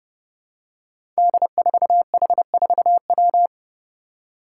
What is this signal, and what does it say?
Morse code, heard as a keyed tone, about 31 wpm, 700 Hz. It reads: D4H4W
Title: Five Challenging Letter-Number Combinations 30wpm